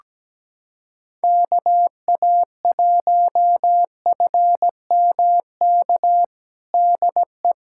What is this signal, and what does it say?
Morse code, heard as a keyed tone, about 17 wpm, 700 Hz. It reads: KA1FMK DE
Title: USA Callsign DE Callsign - 6 Characters - 17wpm